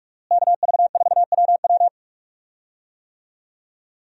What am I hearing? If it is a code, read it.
Morse code, heard as a keyed tone, about 46 wpm, 700 Hz. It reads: KV4WW
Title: USA Call Signs - 5 Characters - Repeated 2x - 45wpm